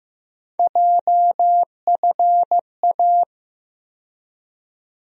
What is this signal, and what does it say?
Morse code, heard as a keyed tone, about 15 wpm, 700 Hz. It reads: JFA
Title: USA Call Sign Postfixes - 3 Letters 15wpm